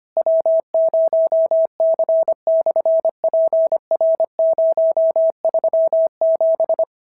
Morse code, heard as a keyed tone, about 25 wpm, 650 Hz. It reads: W0C/PR037
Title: SOTA QSO Practice 25wpm